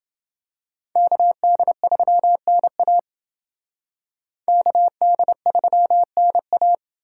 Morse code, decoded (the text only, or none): KD3NA KD3NA